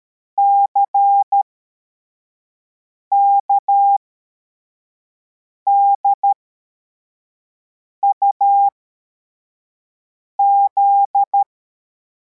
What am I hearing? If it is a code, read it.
Morse code, heard as a keyed tone, about 13 wpm, spaced as for 4 wpm, 800 Hz. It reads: CKDUZ
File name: formation / cw / Text61